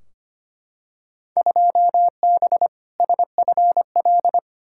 Morse code, heard as a keyed tone, about 25 wpm, 700 Hz. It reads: 2B SFL
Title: Field Day Exchanges - 25wpm